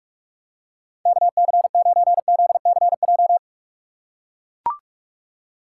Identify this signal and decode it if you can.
Morse code, heard as a keyed tone, about 44 wpm, 700 Hz. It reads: KC9ZCJ
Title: USA Call Signs - 3x Character Spacing - 45wpm